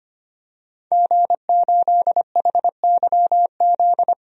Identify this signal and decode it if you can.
Morse code, heard as a keyed tone, about 25 wpm, 700 Hz. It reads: G8HYZ
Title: DXCC-100 Call Signs - 5 Characters - 25wpm